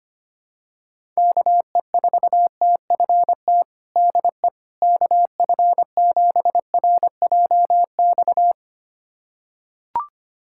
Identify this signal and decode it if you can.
Morse code, heard as a keyed tone, about 25 wpm, 700 Hz. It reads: KE4TFT DE KF7RJX
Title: USA Callsign DE Callsign - 6 Characters - 25wpm